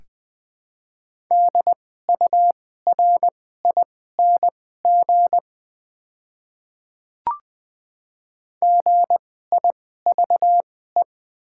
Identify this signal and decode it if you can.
Morse code, heard as a keyed tone, about 20 wpm, spaced as for 15 wpm, 700 Hz. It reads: DURING GIVE
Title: 2x Character Spacing - Top 500 Words - 20wpm